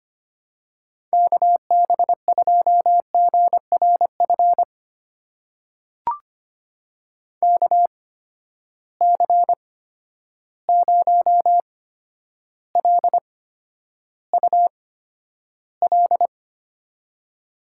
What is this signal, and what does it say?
Morse code, heard as a keyed tone, about 25 wpm, spaced as for 11 wpm, 700 Hz. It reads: KB2GRF KC0LUL
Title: USA Call Signs - 6 Characters - 8x Character Spacing - 25wpm